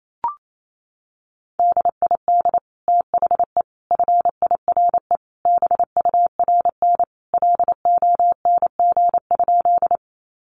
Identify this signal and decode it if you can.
Morse code, heard as a keyed tone, about 28 wpm, 700 Hz. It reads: DID THE FIRE BURN LONG?
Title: Sentences from Top 1300 Words 28wpm